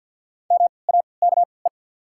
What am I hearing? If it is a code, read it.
Morse code, heard as a keyed tone, about 48 wpm, spaced as for 28 wpm, 700 Hz. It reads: MAKE
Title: 3x Character Spacing - Top 200 Words - 50wpm